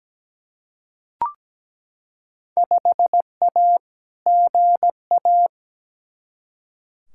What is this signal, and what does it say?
Morse code, heard as a keyed tone, about 17 wpm, 700 Hz. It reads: 5A GA
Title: Field Day Exchanges Encore - 17wpm